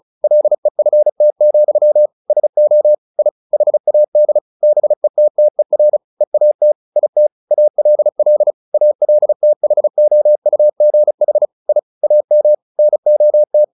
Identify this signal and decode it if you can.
Morse code, heard as a keyed tone, about 34 wpm, 600 Hz. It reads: LEFT, SO I HAD BETTER EAT IT ALL ALTHOUGH I AM NOT
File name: the old man and the sea / the old man and the sea 1 35wpm